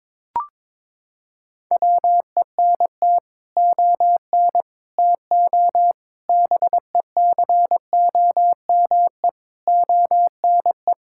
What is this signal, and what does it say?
Morse code, heard as a keyed tone, about 22 wpm, 700 Hz. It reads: WENT ON TO BECOME ONE
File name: practice / 22wpm / Sets of 7 Words - 22wpm